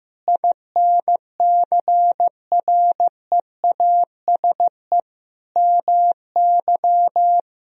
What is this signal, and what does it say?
Morse code, heard as a keyed tone, about 15 wpm, 700 Hz. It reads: INCREASE MY
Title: Sentences from Top 1000 Words - Review Part 4 - 15wpm